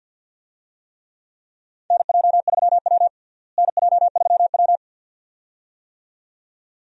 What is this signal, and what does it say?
Morse code, heard as a keyed tone, about 48 wpm, 700 Hz. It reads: NJ3W NJ3W
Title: USA Call Signs - 4 Characters - Repeated 2x - 50wpm